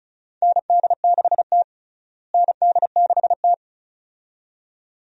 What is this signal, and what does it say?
Morse code, heard as a keyed tone, about 36 wpm, 700 Hz. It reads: ND6T ND6T
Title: USA Call Signs - 4 Characters - Repeated 2x - 35wpm